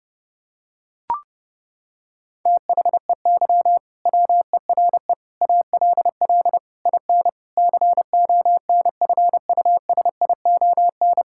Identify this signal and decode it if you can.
Morse code, heard as a keyed tone, about 30 wpm, 700 Hz. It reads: THEY WERE ALL IN CONFUSION
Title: Sentences from Top 1600 Words 30wpm